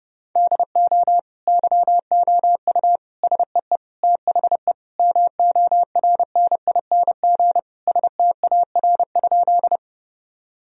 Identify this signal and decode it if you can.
Morse code, heard as a keyed tone, about 30 wpm, 700 Hz. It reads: DO YOU SEE THE MORNING STAR?
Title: Sentences from Top 1000 Words - Review Part 2 - 30wpm